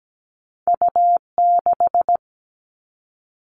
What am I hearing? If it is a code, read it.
Morse code, heard as a keyed tone, about 17 wpm, 700 Hz. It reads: U6
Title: All Cut Numbers - 2-Digit 17wpm